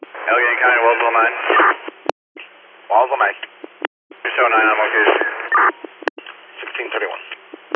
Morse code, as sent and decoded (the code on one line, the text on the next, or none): none